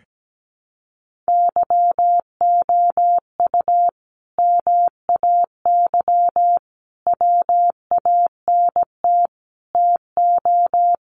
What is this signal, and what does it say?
Morse code, heard as a keyed tone, about 17 wpm, 700 Hz. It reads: YOU MAY WANT TO
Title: Sentences from Top 700 Words 17wpm